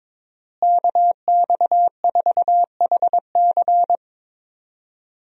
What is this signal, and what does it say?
Morse code, heard as a keyed tone, about 22 wpm, 700 Hz. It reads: KX4HC
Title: USA Call Signs - 5 Characters - Repeated 2x - 22wpm